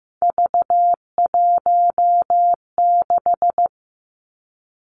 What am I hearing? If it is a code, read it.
Morse code, heard as a keyed tone, about 15 wpm, 700 Hz. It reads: V16
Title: Three Challenging Letter-Number Combinations 15wpm